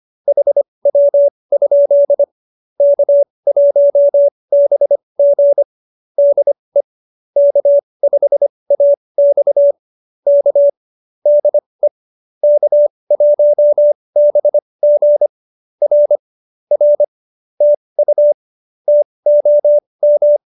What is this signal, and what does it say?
Morse code, heard as a keyed tone, about 25 wpm, spaced as for 20 wpm, 570 Hz. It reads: HW? K1BG DE K5AX K DE K1BG R R TU TOM